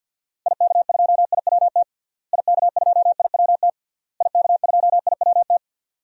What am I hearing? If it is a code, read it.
Morse code, heard as a keyed tone, about 52 wpm, 700 Hz. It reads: IK2IWT IK2IWT IK2IWT
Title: DXCC-100 Call Signs - 6 Characters - Repeated 3x - 50wpm